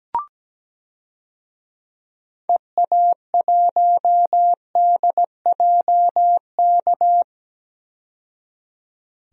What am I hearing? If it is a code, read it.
Morse code, heard as a keyed tone, about 17 wpm, 700 Hz. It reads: EA1DJK
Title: DXCC-100 Call Signs - 6 Characters - 17wpm